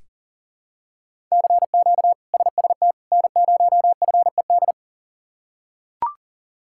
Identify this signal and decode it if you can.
Morse code, heard as a keyed tone, about 40 wpm, 700 Hz. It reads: CQ SST N0FED
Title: K1USN SST QSO Practice - 40wpm